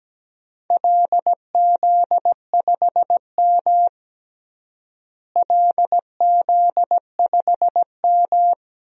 Morse code, heard as a keyed tone, about 17 wpm, 700 Hz. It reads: LZ5M LZ5M
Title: DXCC-100 Call Signs - 4 Characters - Repeated 3x - 17wpm